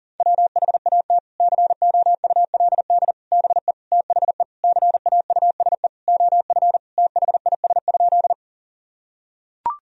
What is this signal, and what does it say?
Morse code, heard as a keyed tone, about 40 wpm, 700 Hz. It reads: WHAT COULD BE THE CAUSE OF THIS?
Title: Sentences from Top 1000 Words - Review Part 1 - 40wpm